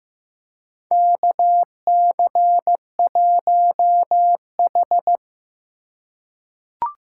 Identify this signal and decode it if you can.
Morse code, heard as a keyed tone, about 15 wpm, 700 Hz. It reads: KC1H
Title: USA Call Signs - 4 Characters - 2x Character Spacing - 15wpm